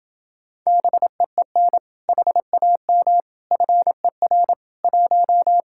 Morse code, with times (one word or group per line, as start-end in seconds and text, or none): 0.7-1.8 BEEN
2.1-3.2 HAM
3.5-4.5 FER
4.8-5.6 1